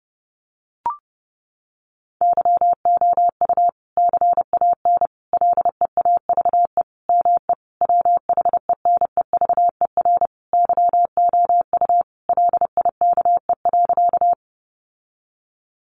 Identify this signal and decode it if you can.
Morse code, heard as a keyed tone, about 30 wpm, 700 Hz. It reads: YOU CAN LEAVE ME WHENEVER YOU LIKE.
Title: Sentences from Top 1200 Words 30wpm